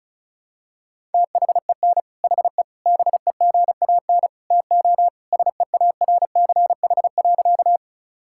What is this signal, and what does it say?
Morse code, heard as a keyed tone, about 36 wpm, 700 Hz. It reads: THEN HE BEGAN TO SEARCH.
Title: Sentences from Top 1000 Words - Review Part 5 - 35wpm